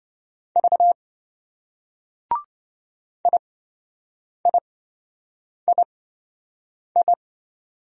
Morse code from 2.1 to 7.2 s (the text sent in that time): IIII